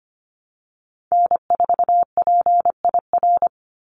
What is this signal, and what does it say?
Morse code, heard as a keyed tone, about 25 wpm, 700 Hz. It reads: N4PIR
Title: Call Signs Repeated 3x - 25wpm